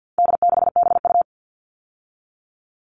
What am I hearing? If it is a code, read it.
Morse code, heard as a keyed tone, about 50 wpm, 700 Hz. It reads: D6BU
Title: All Cut Numbers - 4-Digit 50wpm